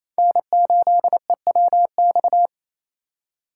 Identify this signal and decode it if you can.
Morse code, heard as a keyed tone, about 28 wpm, 700 Hz. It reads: N8EWX
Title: K1USN SST QSO Practice - 28wpm